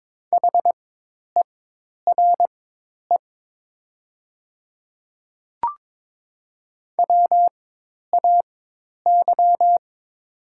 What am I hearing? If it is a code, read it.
Morse code, heard as a keyed tone, about 22 wpm, spaced as for 10 wpm, 700 Hz. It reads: HERE WAY
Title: 4x Character Spacing - Top 300 Words - 22wpm